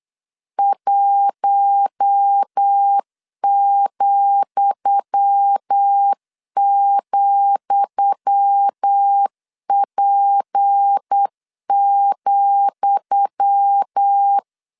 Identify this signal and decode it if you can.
Morse code, heard as a keyed tone, about 8 wpm, 800 Hz. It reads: 1,,P,
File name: LEZIONI / 10 SIGNS / 10D COMA 8wpm 3749